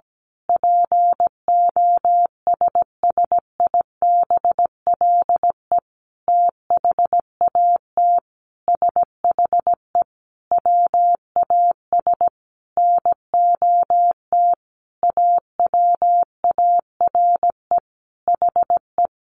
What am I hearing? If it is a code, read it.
Morse code, heard as a keyed tone, about 17 wpm, 700 Hz. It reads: POSSIBLE THAT SHE WAS NOT AWARE HE
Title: Sentences from Top 1200 Words 17wpm